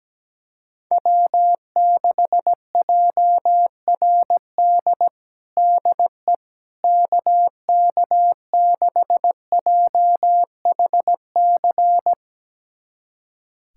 Morse code, heard as a keyed tone, about 17 wpm, 700 Hz. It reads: W6JRD DE KK6JHC
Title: USA Callsign DE Callsign 17wpm